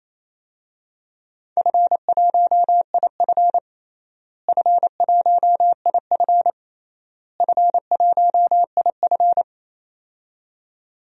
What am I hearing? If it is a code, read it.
Morse code, heard as a keyed tone, about 28 wpm, 700 Hz. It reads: F1IF F1IF F1IF
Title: DXCC-100 Call Signs - 4 Characters - Repeated 3x - 28wpm